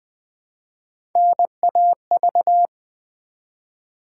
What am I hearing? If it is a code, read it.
Morse code, heard as a keyed tone, about 20 wpm, 700 Hz. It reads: NAV
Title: Build A Word - 1 Letter At A Time - 5-Letter Words - 20wpm